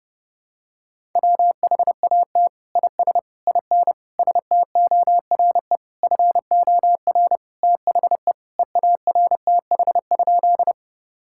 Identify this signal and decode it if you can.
Morse code, heard as a keyed tone, about 32 wpm, 700 Hz. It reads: WHAT IS IN STORE FOR THE EARTH?
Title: Sentences from Top 1000 Words - Review Part 5 - 30wpm